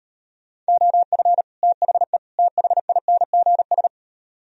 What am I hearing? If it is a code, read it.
Morse code, heard as a keyed tone, about 38 wpm, 700 Hz. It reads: OF THE THINGS